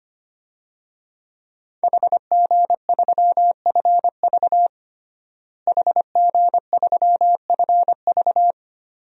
Morse code, heard as a keyed tone, about 25 wpm, 700 Hz. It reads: HG3FV HG3FV
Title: DXCC-100 Call Signs - 5 Characters - Repeated 3x - 25wpm